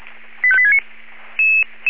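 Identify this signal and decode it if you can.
Five-tone selective call (ZVEI), heard as quick runs of five tones, tones of about 70 ms. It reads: 85869
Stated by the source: ZVEI five-tone selective call received by SDR.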